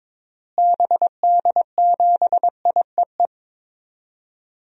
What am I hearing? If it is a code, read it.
Morse code, heard as a keyed tone, about 22 wpm, 700 Hz. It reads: BD7IEE
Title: DXCC-100 Call Signs - 6 Characters - Repeated 2x - 22wpm